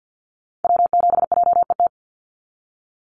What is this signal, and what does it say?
Morse code, heard as a keyed tone, about 50 wpm, 700 Hz. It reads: W7JET
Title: NA SOTA Activator Callsigns 50wpm